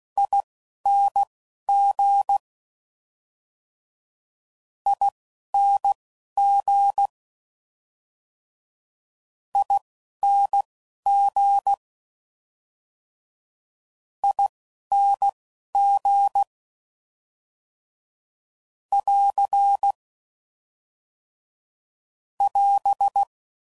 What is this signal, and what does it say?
Morse code, heard as a keyed tone, about 16 wpm, spaced as for 11 wpm, 800 Hz. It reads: ING ING ING ING <AR> <AS>